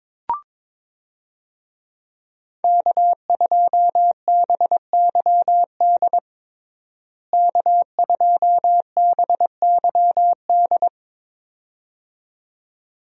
Morse code, heard as a keyed tone, about 21 wpm, 700 Hz. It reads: K2BYD K2BYD
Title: USA Call Signs - 5 Characters - Repeated 2x - 22wpm